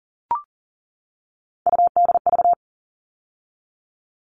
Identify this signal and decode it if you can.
Morse code, heard as a keyed tone, about 40 wpm, 700 Hz. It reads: UDV
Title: All Cut Numbers - 3-Digit 40wpm